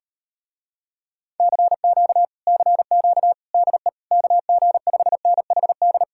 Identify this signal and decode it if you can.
Morse code, heard as a keyed tone, about 33 wpm, 700 Hz. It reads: CQ CQ DE KG5NHD
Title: Speed-Racing Common QSO Phrases Encore - 28wpm